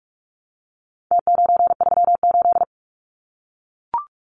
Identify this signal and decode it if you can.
Morse code, heard as a keyed tone, about 44 wpm, 700 Hz. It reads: T938
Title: Numbers - Contest Style - 4-Digit 45wpm